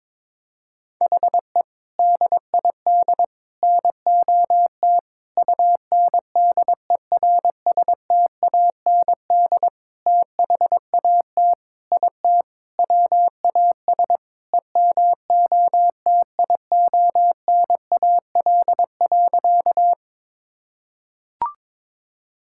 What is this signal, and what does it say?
Morse code, heard as a keyed tone, about 22 wpm, 700 Hz. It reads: HE DID NOT UNDERSTAND THAT IT WAS EMOTIONAL.
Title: Sentences from Top 1000 Words - Review Part 5 - 22wpm